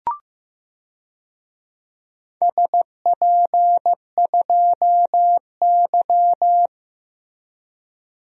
Morse code, heard as a keyed tone, about 15 wpm, 700 Hz. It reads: SP2Y
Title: DXCC-100 Call Signs - 4 Characters - 15wpm